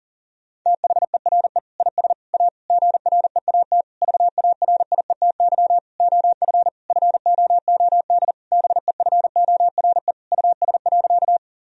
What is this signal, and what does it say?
Morse code, heard as a keyed tone, about 40 wpm, 690 Hz. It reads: THERE IS A GREAT VARIETY OF FOOD BEFORE US.
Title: Sentences from Top 1000 Words - Review Part 4 - 40wpm